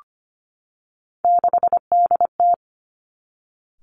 Morse code, heard as a keyed tone, about 25 wpm, 700 Hz. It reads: 6DT
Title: Three Challenging Letter-Number Combinations 25wpm